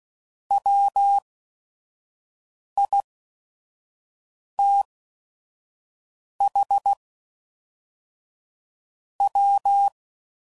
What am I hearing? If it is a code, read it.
Morse code, heard as a keyed tone, about 16 wpm, spaced as for 5 wpm, 800 Hz. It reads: WITH W
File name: CW Code Groups @5WPM / Weather Sets / Weather 4 14